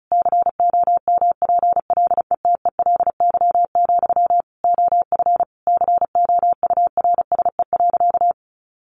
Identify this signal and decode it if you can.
Morse code, heard as a keyed tone, about 34 wpm, 700 Hz. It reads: COMPLETELY, OF COURSE.
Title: Sentences from Top 1100 Words 35wpm